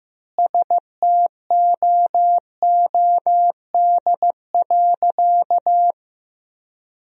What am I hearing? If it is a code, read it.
Morse code, heard as a keyed tone, about 16 wpm, 700 Hz. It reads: STOOD.
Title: 3x Word-Spacing - Sentences from Top 500 Words - 15wpm